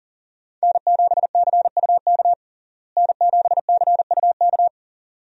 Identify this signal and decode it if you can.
Morse code, heard as a keyed tone, about 40 wpm, 700 Hz. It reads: N7CUK N7CUK
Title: USA Call Signs - 5 Characters - Repeated 2x - 40wpm